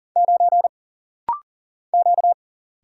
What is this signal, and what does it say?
Morse code, heard as a keyed tone, about 41 wpm, 700 Hz. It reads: 9 Q
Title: Single Letter-Number - Rapid-Fire 40wpm